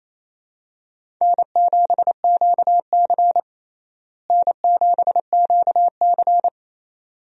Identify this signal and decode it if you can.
Morse code, heard as a keyed tone, about 28 wpm, 700 Hz. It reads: N7QC N7QC
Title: USA Call Signs - 4 Characters - Repeated 2x - 28wpm